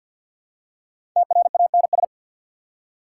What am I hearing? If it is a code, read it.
Morse code, heard as a keyed tone, about 54 wpm, 690 Hz. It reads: TRANS
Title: Prefixes and Suffixes 50wpm